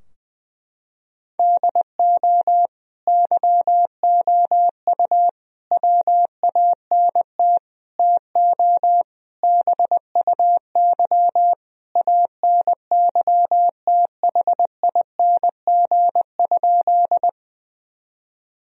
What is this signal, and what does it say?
Morse code, heard as a keyed tone, about 20 wpm, 700 Hz. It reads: DO YOU WANT TO BUY ANYTHING?
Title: Sentences from Top 1000 Words - Review Part 4 - 20wpm